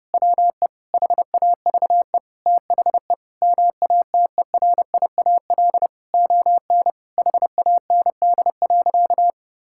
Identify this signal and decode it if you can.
Morse code, heard as a keyed tone, about 30 wpm, 700 Hz. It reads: WE HAVE THE MATERIAL ON HAND.
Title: Sentences from Top 1000 Words - Review Part 5 - 30wpm